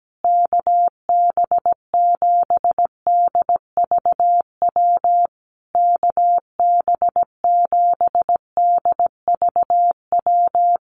Morse code, heard as a keyed tone, about 17 wpm, 700 Hz. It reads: KB7DVW KB7DVW
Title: USA Call Signs Repeated 3x 17wpm